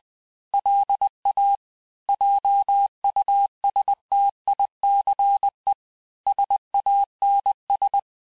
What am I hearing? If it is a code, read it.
Morse code, heard as a keyed tone, about 20 wpm, 800 Hz. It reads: LA JUSTICE SANS
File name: cours-du-mercredi / audios / cours-858-texte-clais-vitesse-19mpm